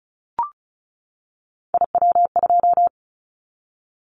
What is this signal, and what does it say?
Morse code, heard as a keyed tone, about 36 wpm, 700 Hz. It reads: IW2
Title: Callsign Prefixes for Top 50 DXCC - 35wpm